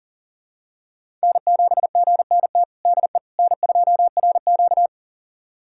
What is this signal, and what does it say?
Morse code, heard as a keyed tone, about 37 wpm, 700 Hz. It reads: N7GNT DE N2RQ
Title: USA Callsign DE Callsign 40wpm